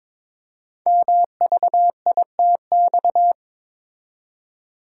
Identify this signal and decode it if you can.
Morse code, heard as a keyed tone, about 22 wpm, 700 Hz. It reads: MVITX
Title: Code Groups - 5 Letters - 22wpm